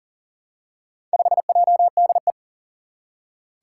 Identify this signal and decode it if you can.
Morse code, heard as a keyed tone, about 40 wpm, 700 Hz. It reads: 5JDE
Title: Four Challenging Letter-Number Combinations 40wpm